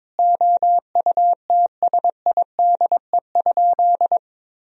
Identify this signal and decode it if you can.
Morse code, heard as a keyed tone, about 22 wpm, 700 Hz. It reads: OUTSIDE?
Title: Sentences from Top 1000 Words - Review Part 4 - 22wpm